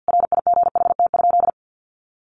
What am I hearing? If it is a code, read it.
Morse code, heard as a keyed tone, about 50 wpm, 700 Hz. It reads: RIGHT?
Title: Sentences from Top 1200 Words 50wpm